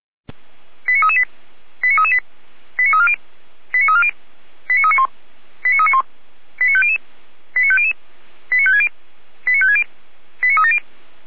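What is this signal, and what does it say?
Five-tone selective call (ZVEI), heard as quick runs of five tones, tones of about 80 ms. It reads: 89338 89338 89340 89340 89391 89391 89500 89500 89560 89560 89370
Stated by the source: ZVEI five-tone call 89304 received by SDR.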